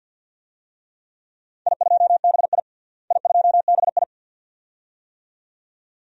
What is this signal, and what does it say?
Morse code, heard as a keyed tone, about 47 wpm, 700 Hz. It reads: I2BI I2BI
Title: DXCC-100 Call Signs - 4 Characters - Repeated 2x - 50wpm